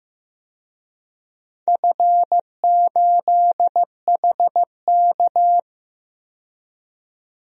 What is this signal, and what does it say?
Morse code, heard as a keyed tone, about 15 wpm, 700 Hz. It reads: F8HK
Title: DXCC-100 Call Signs - 4 Characters - 15wpm